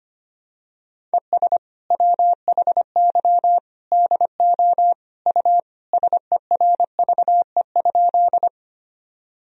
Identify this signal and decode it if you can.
Morse code, heard as a keyed tone, about 25 wpm, 700 Hz. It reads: ES WHY DO U SERVE?
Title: Sentences from Top 400 Words and Abbreviations 25wpm